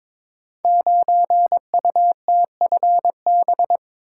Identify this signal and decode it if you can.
Morse code, heard as a keyed tone, about 22 wpm, 700 Hz. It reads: 9UTFB
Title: Code Groups - 5 Letter:Numbers - 22wpm